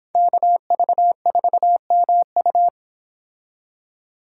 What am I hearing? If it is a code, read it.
Morse code, heard as a keyed tone, about 31 wpm, 700 Hz. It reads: KV4MU